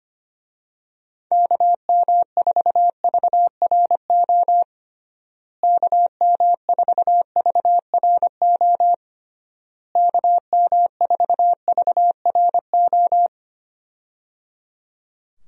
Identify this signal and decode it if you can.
Morse code, heard as a keyed tone, about 25 wpm, 700 Hz. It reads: KM4VRO KM4VRO KM4VRO
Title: USA Call Signs - 6 Characters - Repeated 3x - 25wpm